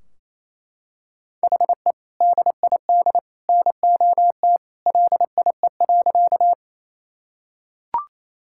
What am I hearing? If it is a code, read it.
Morse code, heard as a keyed tone, about 28 wpm, 700 Hz. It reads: HE DID NOT LIE.
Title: Sentences from Top 1600 Words 28wpm